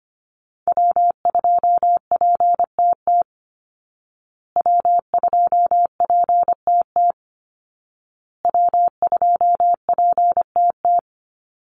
Morse code, heard as a keyed tone, about 25 wpm, 700 Hz. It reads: W2PTT W2PTT W2PTT
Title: Call Signs Repeated 2x - 25wpm